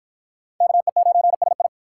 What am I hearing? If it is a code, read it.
Morse code, heard as a keyed tone, about 51 wpm, 700 Hz. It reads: KE9SI